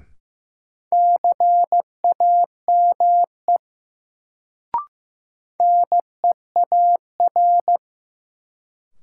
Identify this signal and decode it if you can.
Morse code, heard as a keyed tone, about 16 wpm, 700 Hz. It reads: CAME NEAR
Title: Sentences from Top 200 Words - New Word Drill - 15wpm